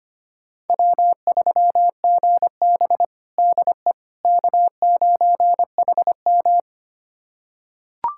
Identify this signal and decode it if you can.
Morse code, heard as a keyed tone, about 25 wpm, 700 Hz. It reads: W3GB DE K9HM
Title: USA Callsign DE Callsign - 4 Characters - 25wpm